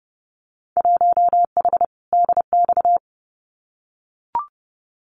Morse code, heard as a keyed tone, about 30 wpm, 700 Hz. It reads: 1H DX
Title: Winter Field Day Exchanges 30wpm